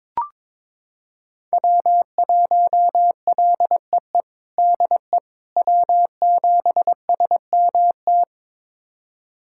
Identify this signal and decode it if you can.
Morse code, heard as a keyed tone, about 22 wpm, 700 Hz. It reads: W1LEE DE W7SMT
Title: USA Callsign DE Callsign - 5 Characters - 22wpm